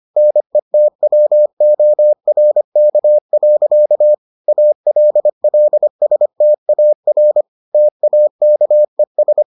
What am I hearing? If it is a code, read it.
Morse code, heard as a keyed tone, about 25 wpm, 600 Hz. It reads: NETWORK. ALLSTAR TAKES